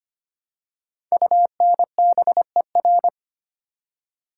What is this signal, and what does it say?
Morse code, heard as a keyed tone, about 25 wpm, 700 Hz. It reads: UNBER
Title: Code Groups - 5 Letters - 25wpm